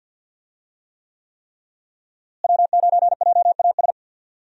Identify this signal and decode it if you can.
Morse code, heard as a keyed tone, about 52 wpm, 700 Hz. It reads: W9JAS
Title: USA Call Signs - 5 Characters - Repeated 3x - 50wpm